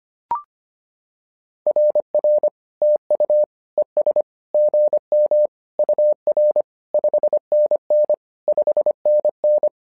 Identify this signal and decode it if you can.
Morse code, heard as a keyed tone, about 25 wpm, 600 Hz. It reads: RR TU ES GM UR 5NN 5NN
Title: SOTA QSO Practice 25wpm